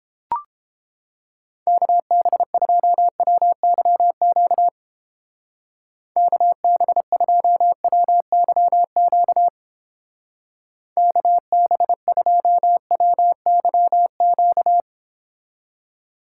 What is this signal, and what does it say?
Morse code, heard as a keyed tone, about 29 wpm, 700 Hz. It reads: KB2WYQ KB2WYQ KB2WYQ